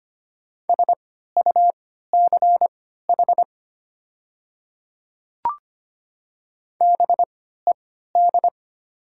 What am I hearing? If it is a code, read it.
Morse code, heard as a keyed tone, about 25 wpm, spaced as for 14 wpm, 700 Hz. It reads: SUCH BED
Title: 3x Character Spacing - Top 500 Words - 25wpm